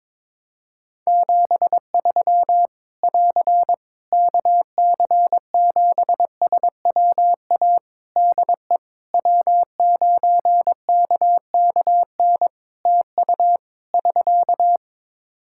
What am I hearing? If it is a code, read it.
Morse code, heard as a keyed tone, about 22 wpm, 700 Hz. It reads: 73 <AR> KC7SWA DE W9KKN TU <SK>
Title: Practice QSO - Exchange 4 - Ending - 22wpm